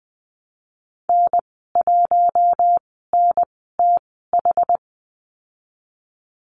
Morse code, heard as a keyed tone, about 20 wpm, spaced as for 15 wpm, 700 Hz. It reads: N1NTH
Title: 231 Lesson 37g - Letter x - Call Signs 20wpm